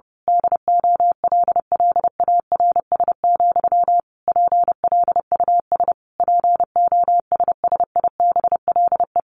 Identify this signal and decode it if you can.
Morse code, heard as a keyed tone, about 30 wpm, 700 Hz. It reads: DOLLARS, PLUS POSSIBLE
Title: A Sound Of Thunder 30wpm